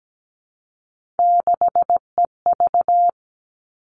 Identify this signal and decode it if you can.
Morse code, heard as a keyed tone, about 17 wpm, 700 Hz. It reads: 6EV
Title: All Cut Numbers - 3-Digit 17wpm